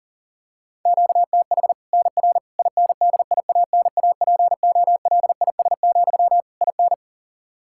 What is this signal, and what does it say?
Morse code, heard as a keyed tone, about 38 wpm, 700 Hz. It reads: QTH NR INDIANAPOLIS, IN
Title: Practice QSO - Exchange 1 - Beginning - 40wpm